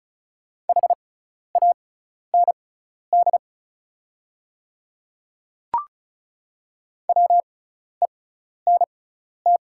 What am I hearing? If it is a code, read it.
Morse code, heard as a keyed tone, about 36 wpm, spaced as for 12 wpm, 690 Hz. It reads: HAND WENT
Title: 6x Character Spacing - Top 200 Words - 35wpm